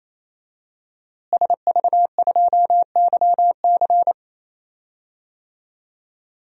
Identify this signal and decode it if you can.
Morse code, heard as a keyed tone, about 27 wpm, 700 Hz. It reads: SV2YC
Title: DXCC-100 Call Signs - 5 Characters - Repeated 3x - 28wpm